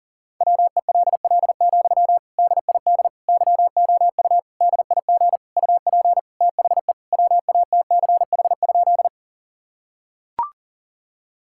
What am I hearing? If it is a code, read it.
Morse code, heard as a keyed tone, about 40 wpm, 700 Hz. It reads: WELL, DID YOU DIG UP THE WATCH?
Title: Sentences from Top 1600 Words 40wpm